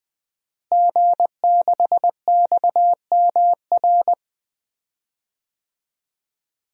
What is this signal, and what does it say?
Morse code, heard as a keyed tone, about 20 wpm, 700 Hz. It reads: G6XMR
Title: DXCC-100 Call Signs - 5 Characters - Repeated 2x - 20wpm